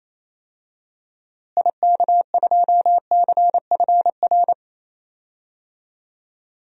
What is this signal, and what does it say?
Morse code, heard as a keyed tone, about 28 wpm, 700 Hz. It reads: IK2CFR
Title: DXCC-100 Call Signs - 6 Characters - Repeated 2x - 28wpm